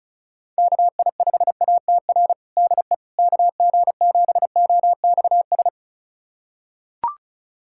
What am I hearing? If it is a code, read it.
Morse code, heard as a keyed tone, about 36 wpm, 700 Hz. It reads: KI5ATR DE KG7OXS
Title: USA Callsign DE Callsign 35wpm